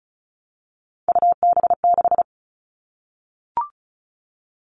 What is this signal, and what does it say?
Morse code, heard as a keyed tone, about 34 wpm, 700 Hz. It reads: UB6
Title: All Cut Numbers - 3-Digit 35wpm